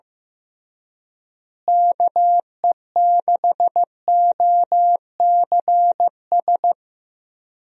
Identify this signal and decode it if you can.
Morse code, heard as a keyed tone, about 15 wpm, 700 Hz. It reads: KE6OCS
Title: USA Call Signs - 6 Characters - Repeated 3x - 15wpm